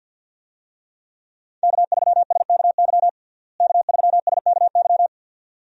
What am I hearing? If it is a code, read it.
Morse code, heard as a keyed tone, about 50 wpm, 700 Hz. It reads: K3SKY K3SKY
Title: USA Call Signs - 5 Characters - Repeated 2x - 50wpm